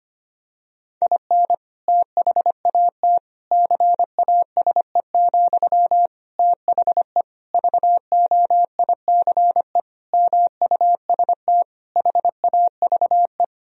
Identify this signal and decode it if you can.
Morse code, heard as a keyed tone, about 25 wpm, 700 Hz. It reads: IN THAT CASE, THE VOICE MUST HAVE
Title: Sentences from Top 1000 Words - Review Part 2 - 25wpm